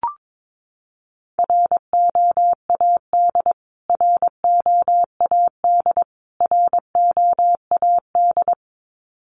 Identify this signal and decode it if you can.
Morse code, heard as a keyed tone, about 22 wpm, 700 Hz. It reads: ROAD ROAD ROAD
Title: Top 500 Words Repeated 22wpm